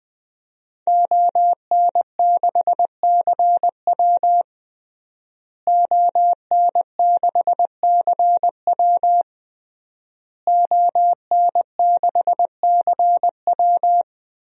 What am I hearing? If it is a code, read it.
Morse code, heard as a keyed tone, about 20 wpm, 700 Hz. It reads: ON6CW ON6CW ON6CW
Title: DXCC-100 Call Signs - 5 Characters - Repeated 3x - 20wpm